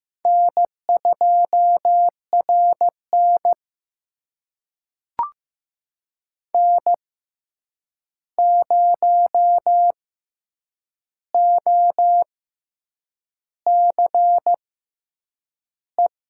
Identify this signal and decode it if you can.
Morse code, heard as a keyed tone, about 15 wpm, spaced as for 7 wpm, 700 Hz. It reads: N2RN N0OCE
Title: USA Call Signs - 6x Character Spacing - 15wpm